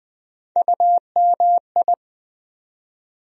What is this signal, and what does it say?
Morse code, heard as a keyed tone, about 20 wpm, 700 Hz. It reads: UMI
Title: USA Call Sign Postfixes - 3 Letters 20wpm